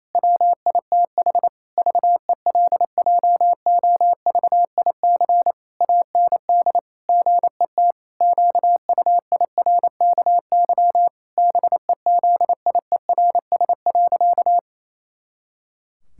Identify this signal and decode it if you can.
Morse code, heard as a keyed tone, about 28 wpm, 700 Hz. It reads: WITH VELJOVIC AND GET QUIRKY BEZIERS.
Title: Pangram Challenge v2 - 28wpm